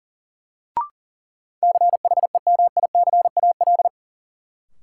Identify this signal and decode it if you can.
Morse code, heard as a keyed tone, about 40 wpm, 700 Hz. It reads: CHEMICAL
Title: Sentences from Top 1200 Words - New Word Drill - 40wpm